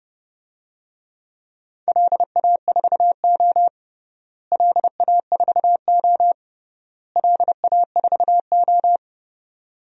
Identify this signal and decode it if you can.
Morse code, heard as a keyed tone, about 30 wpm, 700 Hz. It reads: LA4O LA4O LA4O
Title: DXCC-100 Call Signs - 4 Characters - Repeated 3x - 30wpm